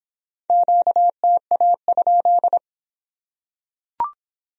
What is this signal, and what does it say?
Morse code, heard as a keyed tone, about 26 wpm, 700 Hz. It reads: QTA?